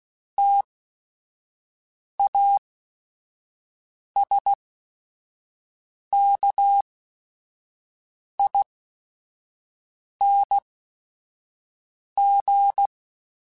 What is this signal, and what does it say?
Morse code, heard as a keyed tone, about 16 wpm, spaced as for 5 wpm, 800 Hz. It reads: TASKING